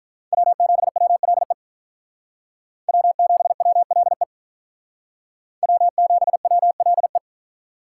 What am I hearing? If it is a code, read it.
Morse code, heard as a keyed tone, about 46 wpm, 700 Hz. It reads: W7WLE W7WLE W7WLE